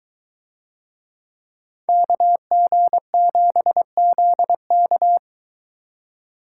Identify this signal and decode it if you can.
Morse code, heard as a keyed tone, about 21 wpm, 700 Hz. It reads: KG7ZK